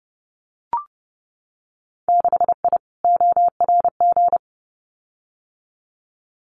Morse code, heard as a keyed tone, about 30 wpm, 700 Hz. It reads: 6I ORG
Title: Winter Field Day Exchanges Encore 30wpm